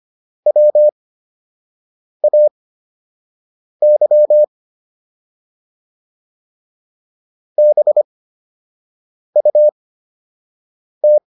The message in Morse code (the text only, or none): WAY BUT